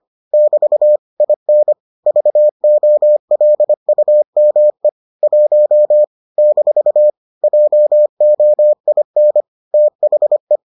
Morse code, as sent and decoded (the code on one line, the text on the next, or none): -...- / .. -. / ...- --- .-.. ..- -- . / .---- / -....- / .--- --- .. -. / - .... .
<BT> IN VOLUME 1 - JOIN THE